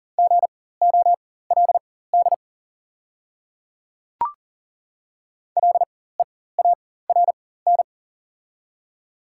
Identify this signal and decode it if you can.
Morse code, heard as a keyed tone, about 40 wpm, spaced as for 19 wpm, 700 Hz. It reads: GOLD LEARN
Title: 4x Character Spacing - Top 500 Words - 40wpm